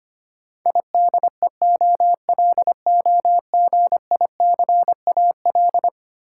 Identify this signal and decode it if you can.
Morse code, heard as a keyed tone, about 25 wpm, 700 Hz. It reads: IDEOLOGICAL
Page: morse-code-ninja-practice-25wpm